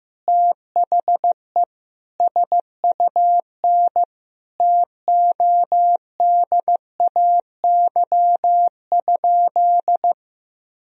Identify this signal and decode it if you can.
Morse code, heard as a keyed tone, about 15 wpm, 700 Hz. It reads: THE SUN TODAY?
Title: Sentences from Top 1000 Words - Review Part 1 - 15wpm